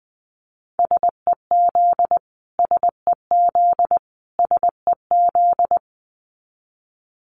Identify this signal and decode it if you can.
Morse code, heard as a keyed tone, about 20 wpm, 700 Hz. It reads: SEZ SEZ SEZ
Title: Abbreviations Repeated 3x - 20wpm